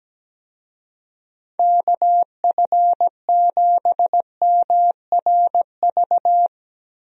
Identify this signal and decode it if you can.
Morse code, heard as a keyed tone, about 17 wpm, 700 Hz. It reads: KF7MRV
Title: USA Call Signs - 6 Characters - Repeated 2x - 17wpm